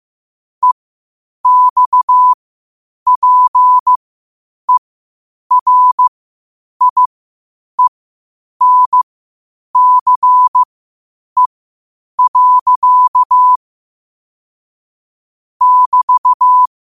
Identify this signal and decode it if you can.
Morse code, heard as a keyed tone, about 15 wpm, spaced as for 8 wpm, 1000 Hz. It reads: EXPERIENCE. <BT>
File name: qotdfiles / 8wpm